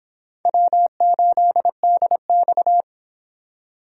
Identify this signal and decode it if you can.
Morse code, heard as a keyed tone, about 30 wpm, 700 Hz. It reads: W8DX